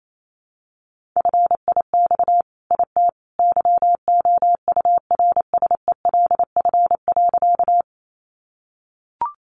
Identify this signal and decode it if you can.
Morse code, heard as a keyed tone, about 28 wpm, 700 Hz. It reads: FIX IT YOURSELF.
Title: Sentences from Top 1400 Words 28wpm